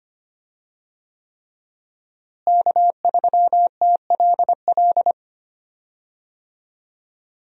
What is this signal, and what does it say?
Morse code, heard as a keyed tone, about 25 wpm, 700 Hz. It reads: K3TLL
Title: USA Call Signs - 5 Characters - Repeated 3x - 25wpm